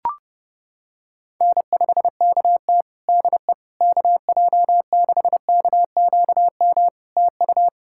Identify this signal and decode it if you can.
Morse code, heard as a keyed tone, about 30 wpm, 700 Hz. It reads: N5KT DE KJ6KQM TU
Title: Practice QSO - Exchange 4 - Ending - 30wpm